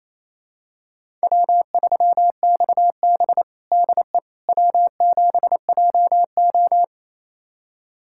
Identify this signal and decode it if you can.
Morse code, heard as a keyed tone, about 28 wpm, 700 Hz. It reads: W3XB DE W7JO
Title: USA Callsign DE Callsign - 4 Characters - 28wpm